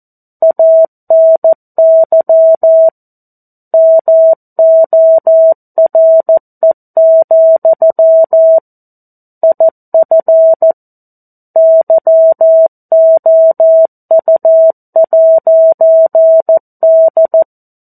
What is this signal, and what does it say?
Morse code, heard as a keyed tone, about 14 wpm, 650 Hz. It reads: ANY MORE, IF YOU'D
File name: 02-Chapter-02 14 WPM